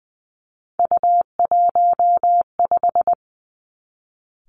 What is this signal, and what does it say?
Morse code, heard as a keyed tone, about 20 wpm, 700 Hz. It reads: U15
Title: Three Challenging Letter-Number Combinations 20wpm